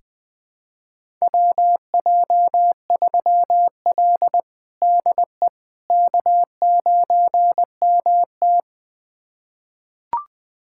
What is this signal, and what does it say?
Morse code, heard as a keyed tone, about 20 wpm, 700 Hz. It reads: WJ3L DE K9MT
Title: USA Callsign DE Callsign - 4 Characters - 20wpm